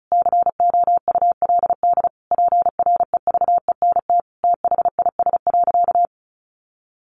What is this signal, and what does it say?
Morse code, heard as a keyed tone, about 34 wpm, 700 Hz. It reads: COULD PREVENT THIS.
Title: Sentences from Top 1100 Words 35wpm